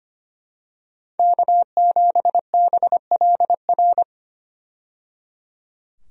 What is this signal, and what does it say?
Morse code, heard as a keyed tone, about 25 wpm, 700 Hz. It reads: K7BLR
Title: USA Call Signs - 5 Characters - 25wpm